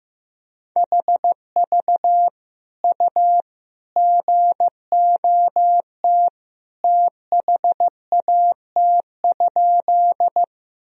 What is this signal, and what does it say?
Morse code, heard as a keyed tone, about 15 wpm, 700 Hz. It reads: HV U GOT THAT?
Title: Sentences from Top 300 Words and Abbreviations 15wpm